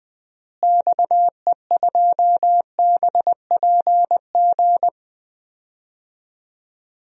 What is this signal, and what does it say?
Morse code, heard as a keyed tone, about 21 wpm, 700 Hz. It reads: XE2BPG
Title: DXCC-100 Call Signs - 6 Characters - Repeated 3x - 20wpm